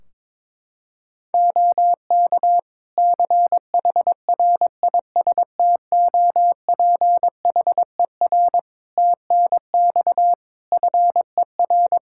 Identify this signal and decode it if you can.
Morse code, heard as a keyed tone, about 22 wpm, 700 Hz. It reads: OK CHRISTOPHER TNX FER
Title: Practice QSO - Exchange 4 - Ending - 22wpm